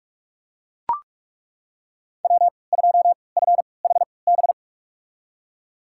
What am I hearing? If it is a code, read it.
Morse code, heard as a keyed tone, about 42 wpm, spaced as for 25 wpm, 700 Hz. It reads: W2FHB
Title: USA Call Signs - 3x Character Spacing - 45wpm